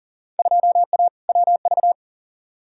Morse code, heard as a keyed tone, about 40 wpm, 700 Hz. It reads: 2A WV
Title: Field Day Exchanges Encore - 40wpm